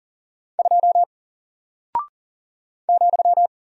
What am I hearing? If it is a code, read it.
Morse code, heard as a keyed tone, about 40 wpm, 700 Hz. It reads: 2 ,
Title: Single Character (Letter:Number:Punctuation:Prosign) Alt 40wpm